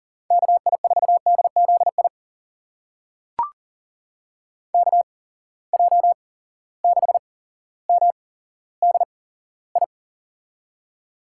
Morse code, heard as a keyed tone, about 40 wpm, spaced as for 17 wpm, 700 Hz. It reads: KI4DZI KJ6MDI
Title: USA Call Signs - 6 Characters - 8x Character Spacing - 40wpm